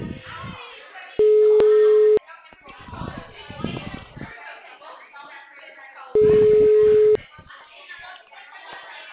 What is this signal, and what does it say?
Ringback tone (Europe, 425 Hz), with speech.